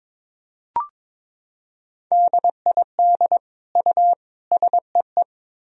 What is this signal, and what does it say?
Morse code, heard as a keyed tone, about 21 wpm, 700 Hz. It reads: DID U SEE
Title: Sentences from Top 300 Words and Abbreviations 22wpm